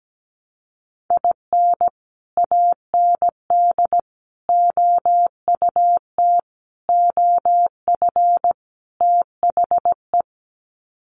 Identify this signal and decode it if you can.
Morse code, heard as a keyed tone, about 17 wpm, 700 Hz. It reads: IN AND OUT OF THE
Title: Sets of 5 Words 17wpm